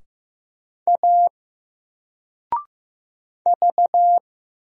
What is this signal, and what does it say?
Morse code, heard as a keyed tone, about 15 wpm, 700 Hz. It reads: A V
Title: Single Character (Letter:Number:Punctuation:Prosign) 15wpm